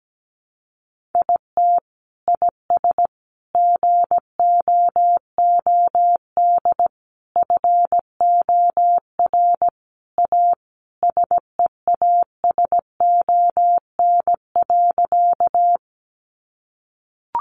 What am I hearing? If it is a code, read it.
Morse code, heard as a keyed tone, about 17 wpm, 700 Hz. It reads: IT IS GOOD FOR A SEASON.
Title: Sentences from Top 1400 Words 17wpm